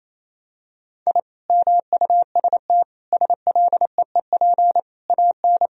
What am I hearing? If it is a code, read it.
Morse code, heard as a keyed tone, about 28 wpm, 700 Hz. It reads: I MUST SLEEP AN
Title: Sentences from Top 1000 Words - Review Part 4 - 28wpm